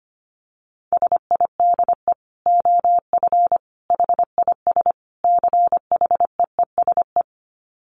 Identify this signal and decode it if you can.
Morse code, heard as a keyed tone, about 25 wpm, 700 Hz. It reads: SIDE OF HIS CHEESE
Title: Dit Challenge 25wpm